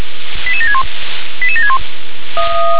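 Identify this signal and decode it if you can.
Five-tone selective call (ZVEI), heard as quick runs of five tones, tones of about 70 ms. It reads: 88761 88761
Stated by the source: Received on SDR, ZVEI five-tone call.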